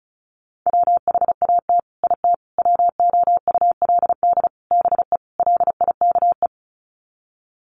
Morse code, heard as a keyed tone, about 34 wpm, 700 Hz. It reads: WHAT IT WOULD BE LIKE
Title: Sets of 5 Words Encore 35wpm